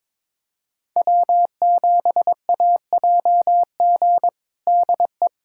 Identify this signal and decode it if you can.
Morse code, heard as a keyed tone, about 21 wpm, 700 Hz. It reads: W7AJG DE
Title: USA Callsign DE Callsign 22wpm